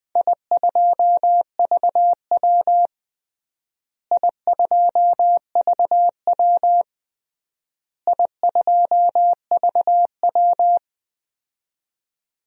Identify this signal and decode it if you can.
Morse code, heard as a keyed tone, about 20 wpm, 700 Hz. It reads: I2VW I2VW I2VW
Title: DXCC-100 Call Signs - 4 Characters - Repeated 3x - 20wpm